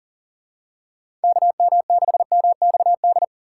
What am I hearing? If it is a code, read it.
Morse code, heard as a keyed tone, about 40 wpm, 700 Hz. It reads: KM6MXD
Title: USA Call Signs - 6 Characters - 2x Character Spacing - 40wpm